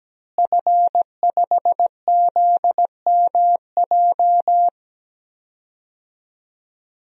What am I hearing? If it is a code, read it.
Morse code, heard as a keyed tone, about 17 wpm, 700 Hz. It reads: F5ZMJ
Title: DXCC-100 Call Signs - 5 Characters - Repeated 2x - 17wpm